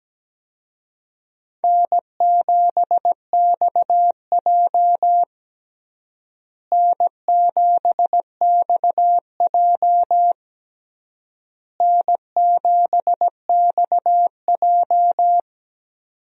Morse code, heard as a keyed tone, about 17 wpm, 700 Hz. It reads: N7XJ N7XJ N7XJ
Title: USA Call Signs - 4 Characters - Repeated 3x - 17wpm